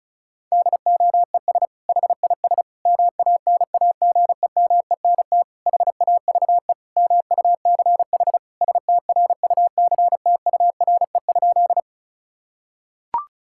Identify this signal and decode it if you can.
Morse code, heard as a keyed tone, about 34 wpm, 700 Hz. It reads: DOES HIS MANAGEMENT HAVE MUCH STRUCTURE?
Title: Sentences from Top 1000 Words - Review Part 6 - 35wpm